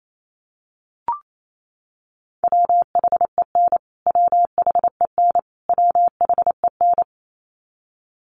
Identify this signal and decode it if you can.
Morse code, heard as a keyed tone, about 28 wpm, 700 Hz. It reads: WHEN WHEN WHEN
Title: Top 500 Words Repeated 28wpm